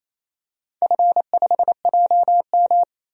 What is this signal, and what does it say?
Morse code, heard as a keyed tone, about 27 wpm, 690 Hz. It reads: F5JM
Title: DXCC-100 Call Signs - 4 Characters - 28wpm